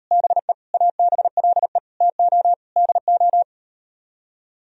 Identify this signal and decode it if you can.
Morse code, heard as a keyed tone, about 38 wpm, 700 Hz. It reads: BE ABLE TO DO